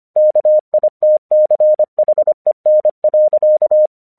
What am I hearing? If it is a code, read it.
Morse code, heard as a keyed tone, about 25 wpm, 610 Hz. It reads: KITCHEN.
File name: English 25wpm